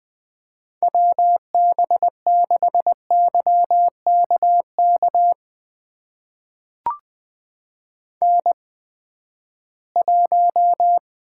Morse code, heard as a keyed tone, about 21 wpm, spaced as for 14 wpm, 700 Hz. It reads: WB6YKK N1
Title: USA Call Signs - 8x Character Spacing - 20wpm